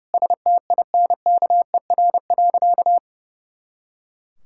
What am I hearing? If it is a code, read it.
Morse code, heard as a keyed tone, about 30 wpm, 700 Hz. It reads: STINKER.
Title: Pangram Challenge v2 - 30wpm